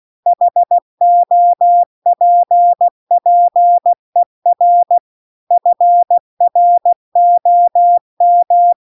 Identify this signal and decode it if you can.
Morse code, heard as a keyed tone, about 16 wpm, 700 Hz. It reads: HOPPER FROM